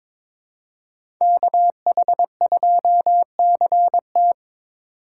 Morse code, heard as a keyed tone, about 22 wpm, 700 Hz. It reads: KH2CT
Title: USA Call Signs - 5 Characters - 4x Character Spacing - 22wpm